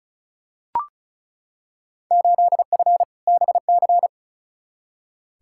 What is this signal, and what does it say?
Morse code, heard as a keyed tone, about 36 wpm, 700 Hz. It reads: 8F BC
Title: Field Day Exchanges Encore - 35wpm